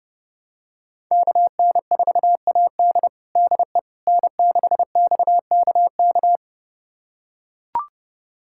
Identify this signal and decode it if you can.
Morse code, heard as a keyed tone, about 30 wpm, 700 Hz. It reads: KN4AD DE N6XKK
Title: USA Callsign DE Callsign - 5 Characters - 30wpm